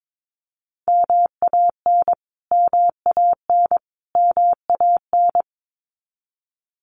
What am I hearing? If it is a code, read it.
Morse code, heard as a keyed tone, about 22 wpm, 700 Hz. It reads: MAN MAN MAN
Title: Top 500 Words Repeated 22wpm